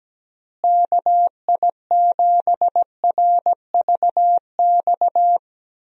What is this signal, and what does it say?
Morse code, heard as a keyed tone, about 17 wpm, 700 Hz. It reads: KI7RVX
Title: USA Call Signs - 6 Characters - Repeated 2x - 17wpm